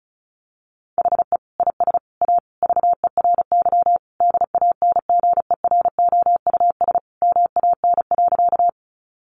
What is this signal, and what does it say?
Morse code, heard as a keyed tone, about 34 wpm, 700 Hz. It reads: HE IS A VERY DANGEROUS MAN.
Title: Sentences from Top 1200 Words 35wpm